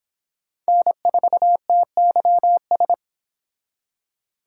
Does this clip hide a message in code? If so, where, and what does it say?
Yes: Morse code from 0.7 to 2.9 s, heard as a keyed tone, about 31 wpm, 700 Hz. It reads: N4TYS